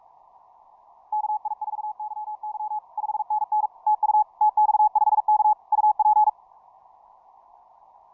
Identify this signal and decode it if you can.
Morse code, heard as a keyed tone, about 46 wpm, 850 Hz. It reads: KI4YY 5NN TU TX5K UP
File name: simple-file-list / tx5k 15mcw2